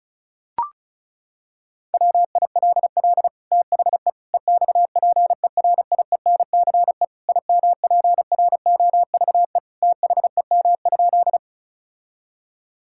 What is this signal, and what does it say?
Morse code, heard as a keyed tone, about 36 wpm, 700 Hz. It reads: WILL THE EXPERIENCE IMPROVE THEM?
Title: Sentences from Top 1000 Words - Review Part 5 - 35wpm